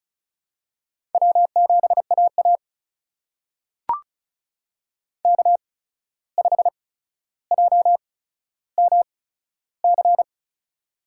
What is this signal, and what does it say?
Morse code, heard as a keyed tone, about 36 wpm, spaced as for 14 wpm, 700 Hz. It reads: W7AA K5JMC
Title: USA Call Signs - 8x Character Spacing - 35wpm